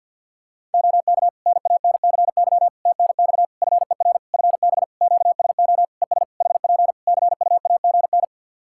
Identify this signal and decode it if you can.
Morse code, heard as a keyed tone, about 50 wpm, 700 Hz. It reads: OK NANCY TNX FER FB QSO ES HP CUAGN
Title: Practice QSO - Exchange 4 - Ending - 50wpm